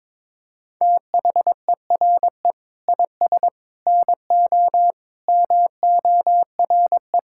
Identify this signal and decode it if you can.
Morse code, heard as a keyed tone, about 22 wpm, 700 Hz. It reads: THERE IS NO MORE
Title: Sentences from Top 1000 Words - Review Part 6 - 22wpm